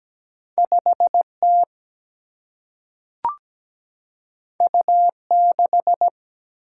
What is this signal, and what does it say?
Morse code, heard as a keyed tone, about 17 wpm, 700 Hz. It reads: 5 T U 6
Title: Single-Letter Single-Number 17wpm